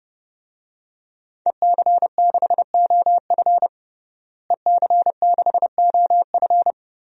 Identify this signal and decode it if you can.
Morse code, heard as a keyed tone, about 32 wpm, 700 Hz. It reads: EC6OF EC6OF
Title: DXCC-100 Call Signs - 5 Characters - Repeated 2x - 30wpm